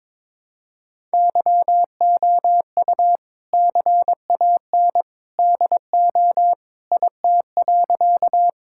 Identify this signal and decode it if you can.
Morse code, heard as a keyed tone, about 22 wpm, 700 Hz. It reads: YOU CAN DO IT.
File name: practice / 22wpm / Sentences with 2-Word Phrases - 22wpm